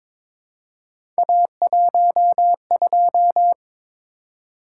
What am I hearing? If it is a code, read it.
Morse code, heard as a keyed tone, about 22 wpm, 700 Hz. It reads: A12
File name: practice / 22wpm / Three Letter-Number Combinations - 22wpm